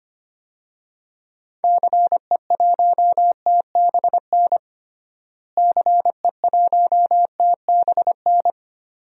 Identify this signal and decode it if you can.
Morse code, heard as a keyed tone, about 25 wpm, 700 Hz. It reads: CE1TBN CE1TBN
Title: DXCC-100 Call Signs - 6 Characters - Repeated 3x - 25wpm